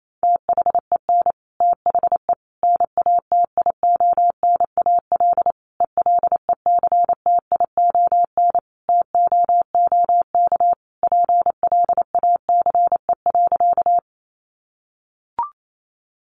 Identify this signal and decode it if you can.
Morse code, heard as a keyed tone, about 28 wpm, 700 Hz. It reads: THEN THE NATIONAL ELECTION TOOK PLACE.
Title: Sentences from Top 1500 Words 28wpm